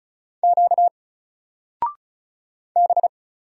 Morse code, heard as a keyed tone, about 36 wpm, 700 Hz. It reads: Q B
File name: Single-Letter-Number-35wpm